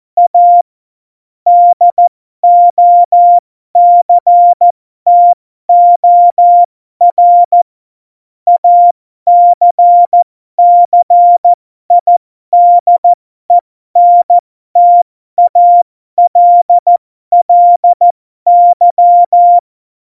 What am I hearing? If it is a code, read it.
Morse code, heard as a keyed tone, about 14 wpm, spaced as for 12 wpm, 700 Hz. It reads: A DOCTOR ACCIDENTALLY
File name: Morsekurs / Witze